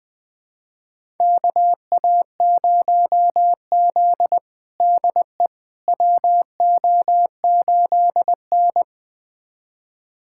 Morse code, heard as a keyed tone, about 20 wpm, 700 Hz. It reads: KA0Z DE WO8N
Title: USA Callsign DE Callsign - 4 Characters - 20wpm